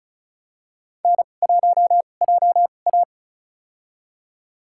Morse code, heard as a keyed tone, about 36 wpm, spaced as for 25 wpm, 700 Hz. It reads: N1JA
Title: USA Call Signs - 4 Characters - 2x Character Spacing - 35wpm